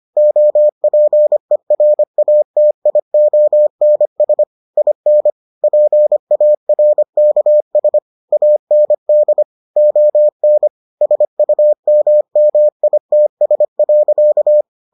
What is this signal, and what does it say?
Morse code, heard as a keyed tone, about 25 wpm, 600 Hz. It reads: OPERATIONS IN PARKS AND ON SUMMITS.